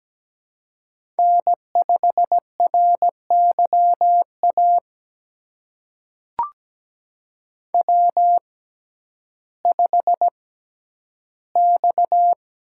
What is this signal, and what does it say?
Morse code, heard as a keyed tone, about 17 wpm, spaced as for 10 wpm, 700 Hz. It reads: N5RYA W5X
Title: USA Call Signs - 6x Character Spacing - 17wpm